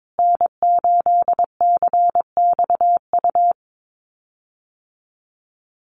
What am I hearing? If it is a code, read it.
Morse code, heard as a keyed tone, about 22 wpm, 700 Hz. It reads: N8CXU
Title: USA Call Signs Repeated 2x 22wpm